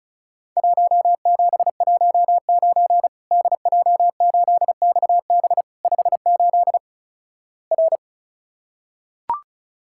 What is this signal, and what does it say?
Morse code, heard as a keyed tone, about 36 wpm, 700 Hz. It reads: 1719 DJ8XB 58 R
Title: Worked All Europe DX Contest - QTC Exchange - 35wpm